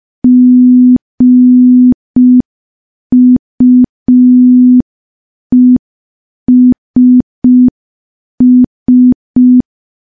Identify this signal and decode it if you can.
Morse code, heard as a keyed tone, about 5 wpm, 250 Hz. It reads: GUESS
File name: morse / morse-training-05WPM